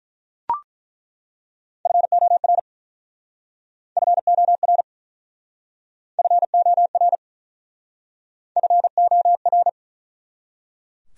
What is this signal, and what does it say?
Morse code, heard as a keyed tone, about 47 wpm, 700 Hz. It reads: FOR FOR FOR FOR